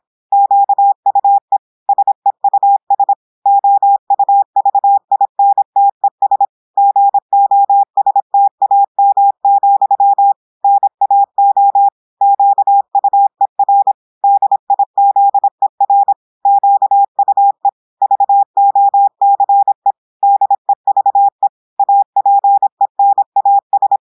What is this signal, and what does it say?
Morse code, heard as a keyed tone, about 26 wpm, 800 Hz. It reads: QUE SEUS OUVINTES GOSTAM, NAO QUER DIZER QUE VOCE DEVE APENAS
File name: portugais texte en clair 2 vitesse 26 mots minutes